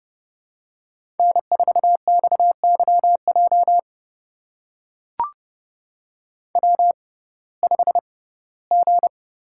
Morse code, heard as a keyed tone, about 30 wpm, spaced as for 18 wpm, 700 Hz. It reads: N4XYJ W5G
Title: USA Call Signs - 6x Character Spacing - 30wpm